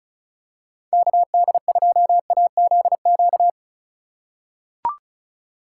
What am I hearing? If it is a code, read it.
Morse code, heard as a keyed tone, about 36 wpm, 700 Hz. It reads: KD2AZQ
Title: USA Call Signs - 6 Characters - 4x Character Spacing - 35wpm